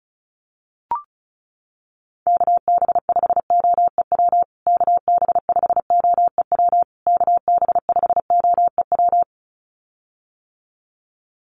Morse code, heard as a keyed tone, about 34 wpm, 700 Hz. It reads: KB5OEW KB5OEW KB5OEW
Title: USA Call Signs Repeated 3x 35wpm